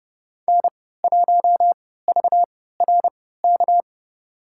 Morse code, heard as a keyed tone, about 30 wpm, spaced as for 17 wpm, 700 Hz. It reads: N1VRK
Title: USA Call Signs - 5 Characters - 3x Character Spacing - 30wpm